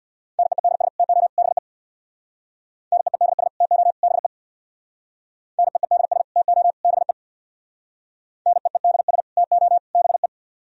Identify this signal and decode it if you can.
Morse code, heard as a keyed tone, about 60 wpm, 700 Hz. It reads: NEEDS TO BE NEEDS TO BE NEEDS TO BE NEEDS TO BE
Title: Speed-Racing - Sets of 3 Words 50wpm